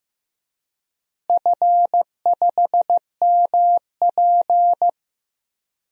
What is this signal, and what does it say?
Morse code, heard as a keyed tone, about 14 wpm, 700 Hz. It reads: F5MP
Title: DXCC-100 Call Signs - 4 Characters - Repeated 2x - 15wpm